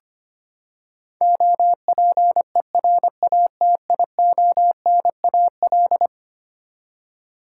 Morse code, heard as a keyed tone, about 25 wpm, 700 Hz. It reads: OPERATIONAL
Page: morse-code-ninja-practice-25wpm